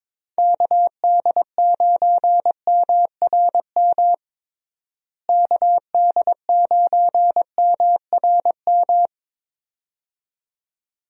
Morse code, heard as a keyed tone, about 21 wpm, 700 Hz. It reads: KD9MRM KD9MRM
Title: USA Call Signs - 6 Characters - Repeated 2x - 22wpm